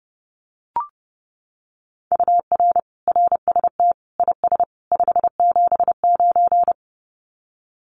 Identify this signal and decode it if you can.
Morse code, heard as a keyed tone, about 30 wpm, 700 Hz. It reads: UR RST IS 579
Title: Common QSO Phrases 30wpm